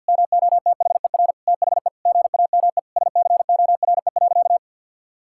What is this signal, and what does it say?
Morse code, heard as a keyed tone, about 50 wpm, 700 Hz. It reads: MOTHER THE GAME SCORE.
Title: Sentences from Top 1600 Words 50wpm